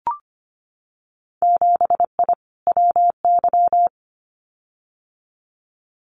Morse code, heard as a keyed tone, about 25 wpm, 700 Hz. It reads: 7I WY
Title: Winter Field Day Exchanges Encore 25wpm